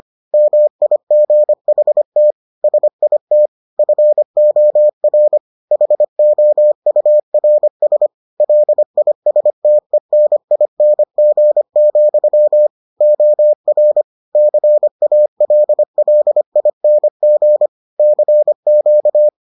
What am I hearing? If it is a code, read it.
Morse code, heard as a keyed tone, about 25 wpm, 600 Hz. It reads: MIGHT SIT FOR HOURS LISTENING, OR CALLING CQ